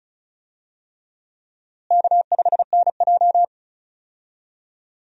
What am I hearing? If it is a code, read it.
Morse code, heard as a keyed tone, about 36 wpm, 700 Hz. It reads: K5NJ
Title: USA Call Signs - 4 Characters - Repeated 3x - 35wpm